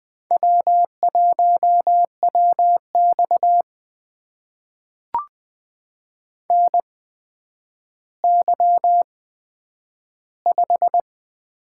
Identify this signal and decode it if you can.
Morse code, heard as a keyed tone, about 20 wpm, spaced as for 10 wpm, 700 Hz. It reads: W1WX NY5
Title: USA Call Signs - 4 Characters - 8x Character Spacing - 20wpm